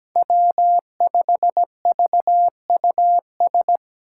Morse code, heard as a keyed tone, about 17 wpm, 700 Hz. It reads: W5VUS
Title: Five Challenging Letter-Number Combinations 17wpm